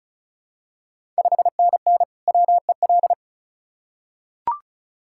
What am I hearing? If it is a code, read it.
Morse code, heard as a keyed tone, about 36 wpm, 700 Hz. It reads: 5NN WEL
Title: Ontario QSO Party Exchange 35wpm